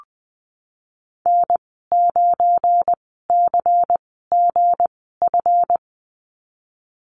Morse code, heard as a keyed tone, about 19 wpm, spaced as for 14 wpm, 700 Hz. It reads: N9CGF
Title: 213 Lesson 34g - Number 8 - Call Signs 20wpm